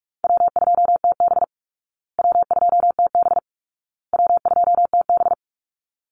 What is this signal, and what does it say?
Morse code, heard as a keyed tone, about 44 wpm, 700 Hz. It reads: W2TB W2TB W2TB
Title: Call Signs Repeated 2x - 45wpm